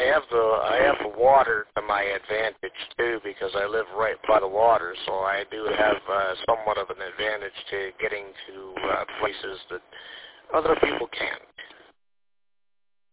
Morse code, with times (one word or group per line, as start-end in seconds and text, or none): none